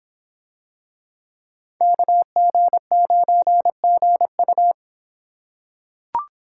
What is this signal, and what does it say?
Morse code, heard as a keyed tone, about 26 wpm, 700 Hz. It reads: KG9GU